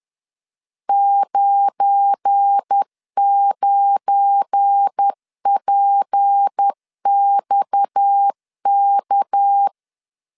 Morse code, heard as a keyed tone, about 10 wpm, 800 Hz. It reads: 99PXK
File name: LEZIONI / 7 GF9Z / 7 GF9 10wpm 2747